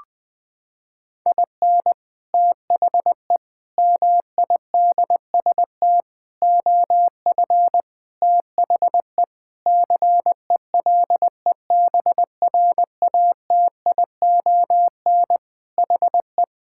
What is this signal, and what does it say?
Morse code, heard as a keyed tone, about 20 wpm, 700 Hz. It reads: IN THE MIDST OF THE CELEBRATION HE
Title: Sentences with 5-Word Phrases - 20wpm